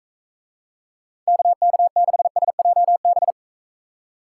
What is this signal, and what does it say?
Morse code, heard as a keyed tone, about 44 wpm, 700 Hz. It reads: KK6SJB